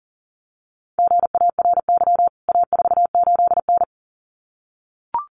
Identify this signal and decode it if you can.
Morse code, heard as a keyed tone, about 40 wpm, 700 Hz. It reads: GARY A48N
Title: CWT Exchange 40wpm